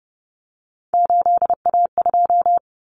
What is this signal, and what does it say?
Morse code, heard as a keyed tone, about 29 wpm, 700 Hz. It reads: 8A2
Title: Common Cut Numbers - 3-Digit 30wpm